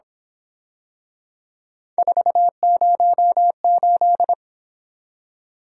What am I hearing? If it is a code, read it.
Morse code, heard as a keyed tone, about 26 wpm, 700 Hz. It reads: 408
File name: Speed-Racing - Three Letter-Number Combinations - 22wpm